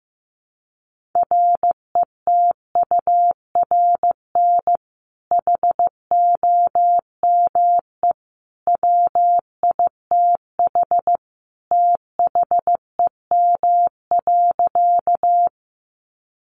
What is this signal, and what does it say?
Morse code, heard as a keyed tone, about 15 wpm, 700 Hz. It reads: RETURN HOME WITH THEM.
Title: Sentences from Top 800 Words 15wpm